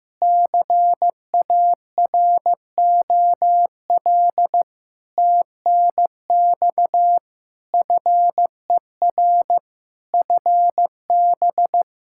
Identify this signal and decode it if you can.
Morse code, heard as a keyed tone, about 15 wpm, 700 Hz. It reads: CAROL TNX FER FB
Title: Practice QSO - Exchange 4 - Ending - 15wpm